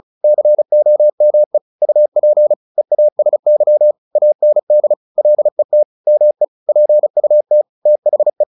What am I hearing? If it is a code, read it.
Morse code, heard as a keyed tone, about 35 wpm, 600 Hz. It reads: COME UP EASY AND LET ME PUT THE
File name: the old man and the sea / the old man and the sea 1 35wpm